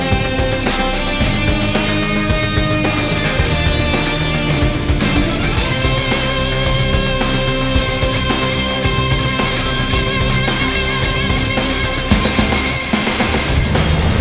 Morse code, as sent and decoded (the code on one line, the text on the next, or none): none